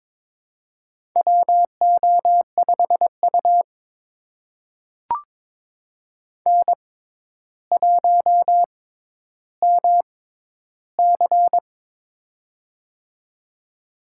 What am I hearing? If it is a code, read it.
Morse code, heard as a keyed tone, about 23 wpm, spaced as for 11 wpm, 700 Hz. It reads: WO5U N1MC
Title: USA Call Signs - 4 Characters - 6x Character Spacing - 22wpm